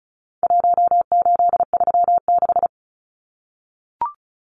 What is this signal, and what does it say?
Morse code, heard as a keyed tone, about 34 wpm, 700 Hz. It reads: 1836
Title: Numbers - Contest Style - 4-Digit 35wpm